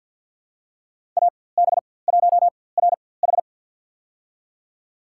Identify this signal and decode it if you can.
Morse code, heard as a keyed tone, about 48 wpm, spaced as for 23 wpm, 700 Hz. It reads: AB1RH
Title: USA Call Signs - 5 Characters - 4x Character Spacing - 50wpm